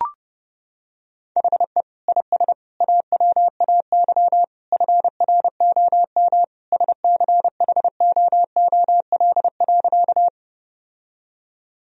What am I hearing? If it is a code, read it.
Morse code, heard as a keyed tone, about 30 wpm, 700 Hz. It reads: HE IS AWAY FROM SCHOOL.
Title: Sentences from Top 1000 Words - Review Part 1 - 30wpm